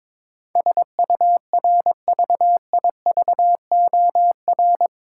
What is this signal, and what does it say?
Morse code, heard as a keyed tone, about 22 wpm, 700 Hz. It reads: SURVIVOR
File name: practice / 22wpm / Eight Letter Words - 22wpm